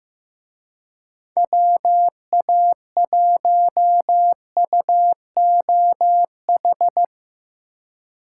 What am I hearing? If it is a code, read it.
Morse code, heard as a keyed tone, about 16 wpm, 700 Hz. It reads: WA1UOH
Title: Speed Racing - USA 2x3 Call Signs 15wpm